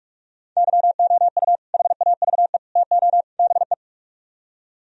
Morse code, heard as a keyed tone, about 58 wpm, 700 Hz. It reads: YOU HAVE TO BE